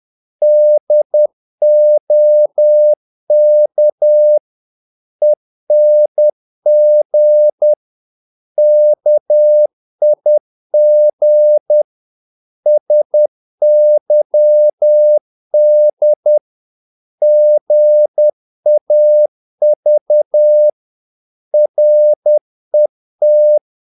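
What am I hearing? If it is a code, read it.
Morse code, heard as a keyed tone, about 10 wpm, 600 Hz. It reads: DOK ENG KIG SYD GAV RET
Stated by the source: Morse practice of short Danish words at 10 wpm, from CW med Gnister.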